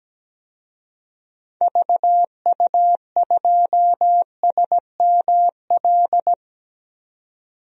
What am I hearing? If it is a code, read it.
Morse code, heard as a keyed tone, about 17 wpm, 700 Hz. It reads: VU2SML
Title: DXCC-100 Call Signs - 6 Characters - 17wpm